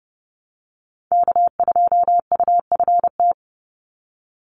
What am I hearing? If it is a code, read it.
Morse code, heard as a keyed tone, about 30 wpm, 700 Hz. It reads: K2UFT
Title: Call Signs - 30wpm